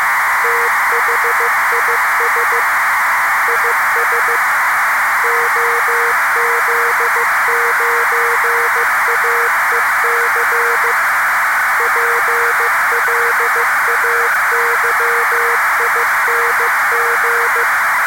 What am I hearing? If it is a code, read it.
Morse code, heard as a keyed tone, about 15 wpm, 470 Hz, with noise. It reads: THIS IS OZ9AEC PLAYING
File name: posts / arissat-bpsk-1000-data-transmission-with-cw-message